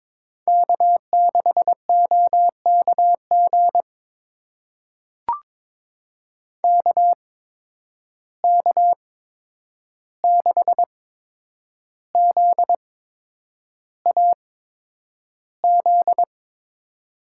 Morse code, heard as a keyed tone, about 22 wpm, spaced as for 9 wpm, 700 Hz. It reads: K6OKG KK6ZAZ
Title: USA Call Signs - 8x Character Spacing - 22wpm